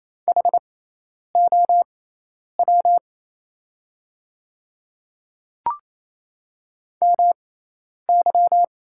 Morse code, heard as a keyed tone, about 28 wpm, spaced as for 10 wpm, 700 Hz. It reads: HOW MY
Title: 6x Character Spacing - Top 100 Words - 28wpm